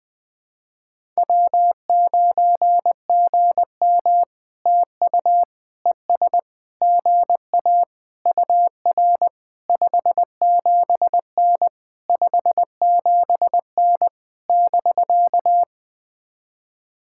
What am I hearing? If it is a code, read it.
Morse code, heard as a keyed tone, about 20 wpm, 700 Hz. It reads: W9GM TU ES GA UR 57N 57N <BK>
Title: POTA QSO Practice - 20wpm